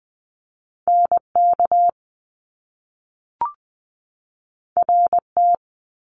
Morse code, heard as a keyed tone, about 20 wpm, 700 Hz. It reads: NK RT
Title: Top Two Letter Combinations in Words 20wpm